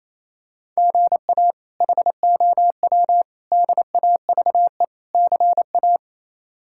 Morse code, heard as a keyed tone, about 27 wpm, 700 Hz. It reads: GA HOW DAVE CA
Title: K1USN SST QSO Practice - 28wpm